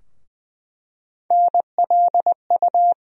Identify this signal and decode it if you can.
Morse code, heard as a keyed tone, about 20 wpm, 700 Hz. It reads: NLU
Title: USA Call Sign Postfixes - 3 Letters 20wpm